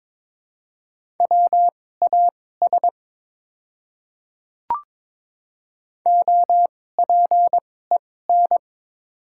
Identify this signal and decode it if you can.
Morse code, heard as a keyed tone, about 22 wpm, spaced as for 16 wpm, 700 Hz. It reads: WAS OPEN
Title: 2x Character Spacing - Top 300 Words - 22wpm